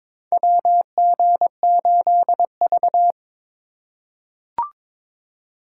Morse code, heard as a keyed tone, about 21 wpm, 700 Hz. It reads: WG8V
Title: USA Call Signs - 4 Characters - 2x Character Spacing - 22wpm